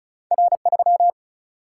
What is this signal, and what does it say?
Morse code, heard as a keyed tone, about 36 wpm, 700 Hz. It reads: R 3
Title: Single-Letter Single-Number 35wpm